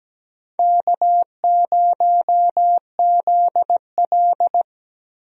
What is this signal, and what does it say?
Morse code, heard as a keyed tone, about 17 wpm, 700 Hz. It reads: K0ZL
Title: USA Call Signs - 4 Characters - Repeated 2x - 17wpm